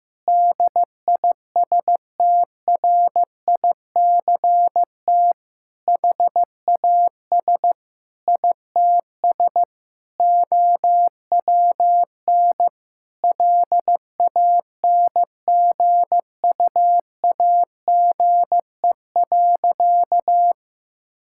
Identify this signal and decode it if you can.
Morse code, heard as a keyed tone, about 14 wpm, 700 Hz. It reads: DISTRICT HAS ITS OWN LANGUAGE.
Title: Sentences from Top 1600 Words 15wpm